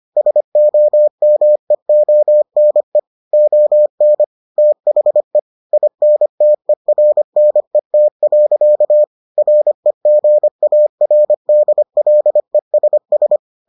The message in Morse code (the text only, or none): SOMEONE ON THE INTERNET. REGARDLESS